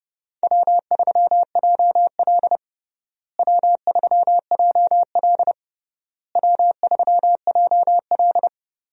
Morse code, heard as a keyed tone, about 32 wpm, 700 Hz. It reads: W3JL W3JL W3JL
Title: USA Call Signs - 4 Characters - Repeated 3x - 30wpm